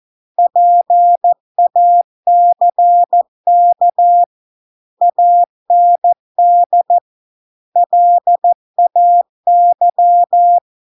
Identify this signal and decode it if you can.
Morse code, heard as a keyed tone, about 14 wpm, 700 Hz. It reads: PACK AND LAY